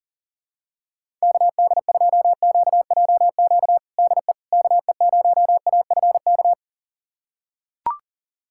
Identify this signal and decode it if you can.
Morse code, heard as a keyed tone, about 40 wpm, 700 Hz. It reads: KD2QJQ DE KE0AFK
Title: USA Callsign DE Callsign 40wpm